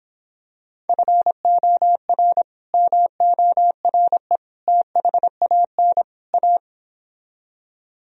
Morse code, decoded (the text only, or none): FOR MORE THAN A